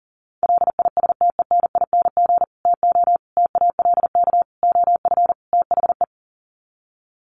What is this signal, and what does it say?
Morse code, heard as a keyed tone, about 40 wpm, 700 Hz. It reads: LISTENING TO TALK OF THE
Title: Sets of 5 Words 40wpm